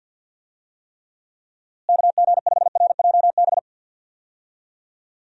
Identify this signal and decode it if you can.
Morse code, heard as a keyed tone, about 52 wpm, 700 Hz. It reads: KG5RJB
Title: USA Call Signs - 6 Characters - Repeated 3x - 50wpm